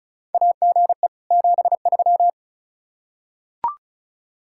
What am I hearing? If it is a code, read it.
Morse code, heard as a keyed tone, about 36 wpm, 700 Hz. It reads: AGE 73
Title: Practice QSO - Exchange 3 - More Info - 35wpm